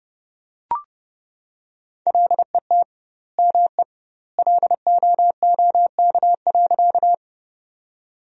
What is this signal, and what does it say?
Morse code, heard as a keyed tone, about 32 wpm, 700 Hz. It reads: LET ME LOOK.
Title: 2x Word-Spacing - Sentences from Top 300 Words - 30wpm